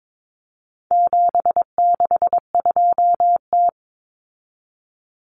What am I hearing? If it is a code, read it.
Morse code, heard as a keyed tone, about 22 wpm, 700 Hz. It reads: 762T
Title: Common Cut Numbers - 4-Digit 22wpm